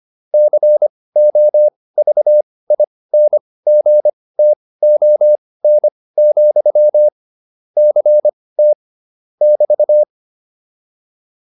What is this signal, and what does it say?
Morse code, heard as a keyed tone, about 25 wpm, spaced as for 18 wpm, 600 Hz. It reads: COVINGTON, CT <BT>
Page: intermediate-practice-files